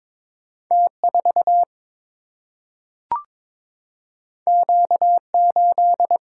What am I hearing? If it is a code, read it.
Morse code, heard as a keyed tone, about 22 wpm, 700 Hz. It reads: T 4 Q 8
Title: Single-Letter Single-Number 22wpm